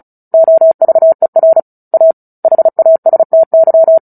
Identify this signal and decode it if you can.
Morse code, heard as a keyed tone, about 34 wpm, 650 Hz. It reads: OVER A HASTY
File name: Book Two / 09-Chapter 9 - 35 WPM